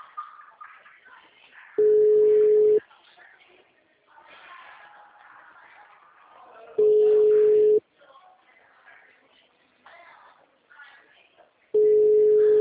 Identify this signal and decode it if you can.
Ringback tone (Europe, 425 Hz), with speech.
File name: out- -2010- - - .49763